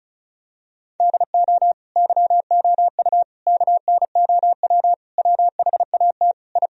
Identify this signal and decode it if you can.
Morse code, heard as a keyed tone, about 34 wpm, 700 Hz. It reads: DO YOU KNOW WHAT I
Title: Sentences from Top 1000 Words - Review Part 4 - 35wpm